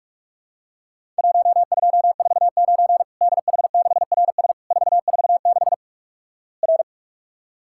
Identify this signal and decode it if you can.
Morse code, heard as a keyed tone, about 44 wpm, 700 Hz. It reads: 1249 DH6RS 446 R
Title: Worked All Europe DX Contest - QTC Exchange - 45wpm